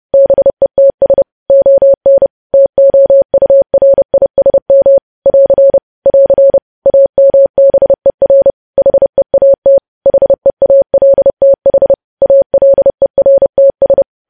Morse code, heard as a keyed tone, about 30 wpm, 550 Hz. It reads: BETS ON TOURISM <AR> <AR> AMBER HEAT HEALTH ALERTS
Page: News Headlines in Morse Code at 30 WPM